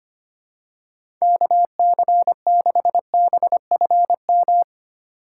Morse code, heard as a keyed tone, about 25 wpm, 700 Hz. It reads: KC6BFM
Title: USA Call Signs - 6 Characters - 2x Character Spacing - 25wpm